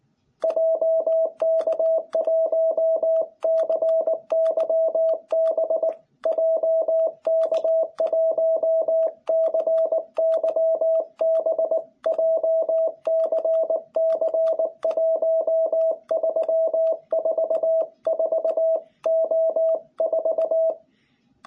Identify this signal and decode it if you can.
Morse code, heard as a keyed tone, about 19 wpm, 660 Hz. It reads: JK1CYB JK1CYB JCC1344O4
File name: jh8cbh / 2021cw